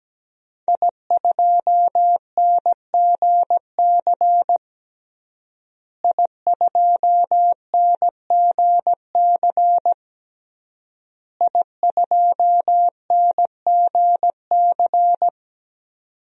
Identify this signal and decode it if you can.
Morse code, heard as a keyed tone, about 17 wpm, 700 Hz. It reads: I2NGC I2NGC I2NGC
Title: DXCC-100 Call Signs - 5 Characters - Repeated 3x - 17wpm